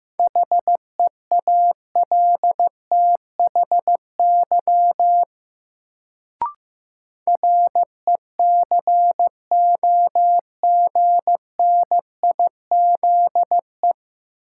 Morse code, heard as a keyed tone, about 15 wpm, 700 Hz. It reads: HEALTHY RECOGNIZE
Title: Sentences from Top 1400 Words - New Word Drill Alt - 15wpm